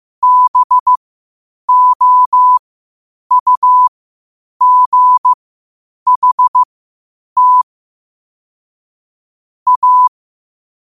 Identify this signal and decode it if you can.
Morse code, heard as a keyed tone, about 15 wpm, spaced as for 8 wpm, 1000 Hz. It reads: BOUGHT A
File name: qotdfiles / 8wpm